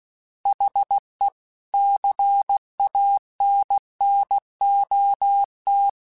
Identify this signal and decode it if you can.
Morse code, heard as a keyed tone, about 16 wpm, 800 Hz. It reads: HE CANNOT
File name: Chuckles / Chuckles-mp3s / Glass Half Full @15